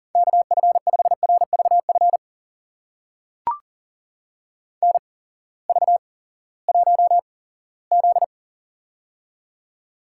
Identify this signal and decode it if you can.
Morse code, heard as a keyed tone, about 44 wpm, spaced as for 20 wpm, 700 Hz. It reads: KF5RVF NV1Z
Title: USA Call Signs - 8x Character Spacing - 40wpm